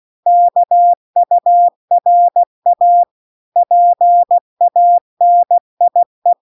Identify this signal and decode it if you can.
Morse code, heard as a keyed tone, about 16 wpm, 700 Hz. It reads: KURA PANIE